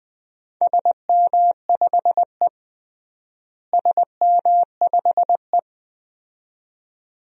Morse code, heard as a keyed tone, about 21 wpm, 700 Hz. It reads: SM5E SM5E
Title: DXCC-100 Call Signs - 4 Characters - Repeated 2x - 20wpm